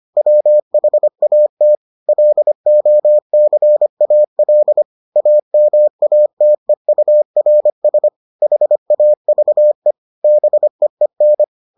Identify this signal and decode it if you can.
Morse code, heard as a keyed tone, about 25 wpm, 600 Hz. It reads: WHAT LOCAL AMATEURS HAVE BEEN